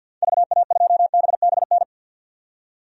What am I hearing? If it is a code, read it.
Morse code, heard as a keyed tone, about 50 wpm, 700 Hz. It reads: VA2BBN
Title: DXCC-100 Call Signs - 6 Characters - 50wpm